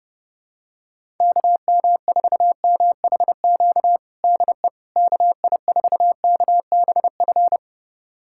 Morse code, heard as a keyed tone, about 32 wpm, 700 Hz. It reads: KM4MHQ DE KI4KBF
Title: USA Callsign DE Callsign - 6 Characters - 30wpm